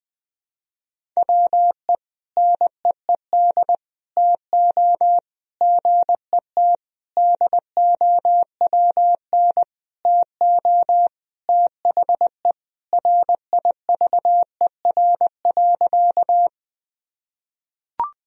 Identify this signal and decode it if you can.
Morse code, heard as a keyed tone, about 19 wpm, 700 Hz. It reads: WE NEED TO GET DOWN TO THE RIVER.
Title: Sentences from Top 1000 Words - Review Part 1 - 20wpm